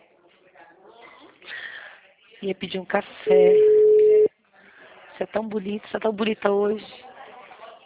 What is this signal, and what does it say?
Ringback tone (Europe, 425 Hz), with speech.